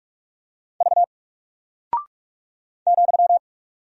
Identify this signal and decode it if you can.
Morse code, heard as a keyed tone, about 44 wpm, 700 Hz. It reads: V ,
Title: Single Character (Letter:Number:Punctuation:Prosign) 45wpm